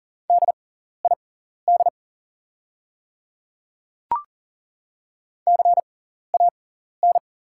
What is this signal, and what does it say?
Morse code, heard as a keyed tone, about 39 wpm, spaced as for 14 wpm, 690 Hz. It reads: DID CAN
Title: 6x Character Spacing - Top 100 Words - 40wpm